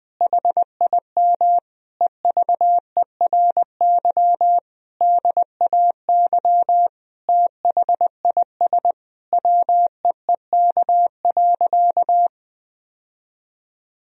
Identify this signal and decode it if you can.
Morse code, heard as a keyed tone, about 20 wpm, 700 Hz. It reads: HIM EVERY DAY THIS WEEK.
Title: Sentences from Top 1000 Words - Review Part 2 - 20wpm